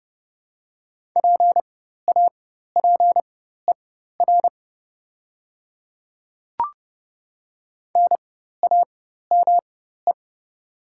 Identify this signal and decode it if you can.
Morse code, heard as a keyed tone, about 30 wpm, spaced as for 14 wpm, 700 Hz. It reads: PAPER NAME
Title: 4x Character Spacing - Top 300 Words - 30wpm